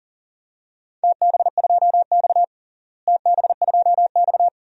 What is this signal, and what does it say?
Morse code, heard as a keyed tone, about 40 wpm, 700 Hz. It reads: TB2X TB2X
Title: DXCC-100 Call Signs - 4 Characters - Repeated 2x - 40wpm